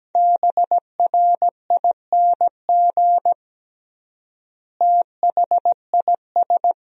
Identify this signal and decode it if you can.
Morse code, heard as a keyed tone, about 17 wpm, 700 Hz. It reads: BRING THIS
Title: 3x Word-Spacing - Sentences from Top 500 Words - 17wpm